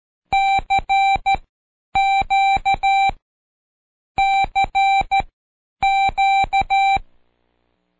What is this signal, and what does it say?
Morse code, heard as a keyed tone, about 13 wpm, spaced as for 9 wpm, 780 Hz. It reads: CQ CQ
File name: Code Quick / AudioDownld1 / Seg52-Day6-Pract Q